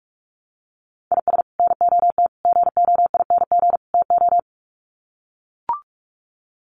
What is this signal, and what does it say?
Morse code, heard as a keyed tone, about 44 wpm, 700 Hz. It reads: IS NOT GOING TO
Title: Sets of 4 Words 45wpm